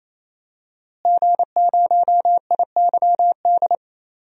Morse code, heard as a keyed tone, about 28 wpm, 700 Hz. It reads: G0IYD
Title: DXCC-100 Call Signs - 5 Characters - Repeated 3x - 28wpm